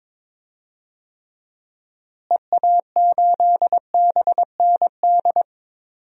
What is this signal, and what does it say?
Morse code, heard as a keyed tone, about 21 wpm, 700 Hz. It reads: EA8BND
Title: DXCC-100 Call Signs - 6 Characters - Repeated 3x - 22wpm